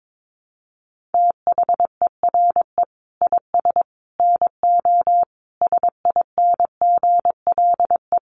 Morse code, heard as a keyed tone, about 22 wpm, 700 Hz. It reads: THERE IS NO SINGLE
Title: Sentences from Top 1300 Words 22wpm